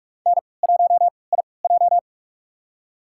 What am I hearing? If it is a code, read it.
Morse code, heard as a keyed tone, about 46 wpm, spaced as for 26 wpm, 700 Hz. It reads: N1IJ
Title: USA Call Signs - 4 Characters - 3x Character Spacing - 45wpm